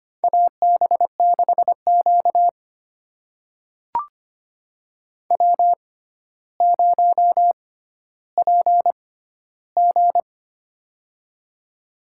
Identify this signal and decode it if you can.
Morse code, heard as a keyed tone, about 25 wpm, spaced as for 13 wpm, 700 Hz. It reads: AB6Q W0PG
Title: USA Call Signs - 4 Characters - 6x Character Spacing - 25wpm